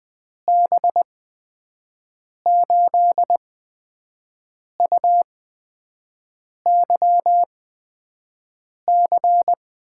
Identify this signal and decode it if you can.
Morse code, heard as a keyed tone, about 20 wpm, spaced as for 5 wpm, 700 Hz. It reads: B8UYC
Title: USA Call Signs - 6 Characters - 8x Character Spacing - 20wpm